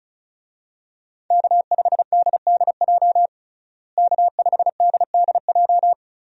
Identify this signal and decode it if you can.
Morse code, heard as a keyed tone, about 34 wpm, 700 Hz. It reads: K5DDJ K5DDJ
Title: USA Call Signs - 5 Characters - Repeated 2x - 35wpm